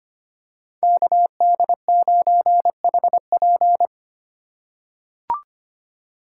Morse code, heard as a keyed tone, about 25 wpm, 700 Hz. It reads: KD9HP
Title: USA Call Signs - 5 Characters - 3x Character Spacing - 25wpm